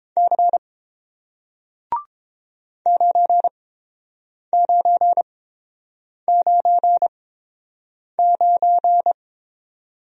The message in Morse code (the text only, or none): C 9999